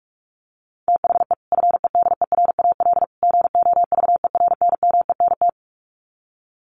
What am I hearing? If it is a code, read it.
Morse code, heard as a keyed tone, about 44 wpm, 700 Hz. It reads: THE FEDERAL GOVERNMENT
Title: Sets of 3 Words 45wpm